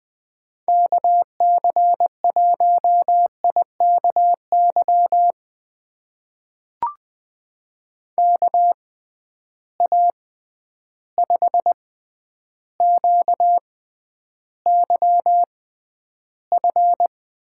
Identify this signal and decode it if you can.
Morse code, heard as a keyed tone, about 20 wpm, spaced as for 10 wpm, 700 Hz. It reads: KC1IKY KA5QYF
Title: USA Call Signs - 6 Characters - 6x Character Spacing - 20wpm